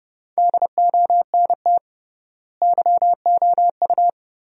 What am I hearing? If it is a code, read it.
Morse code, heard as a keyed tone, about 30 wpm, 700 Hz. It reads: DONT YOU
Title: 3x Word-Spacing - Sentences from Top 300 Words - 30wpm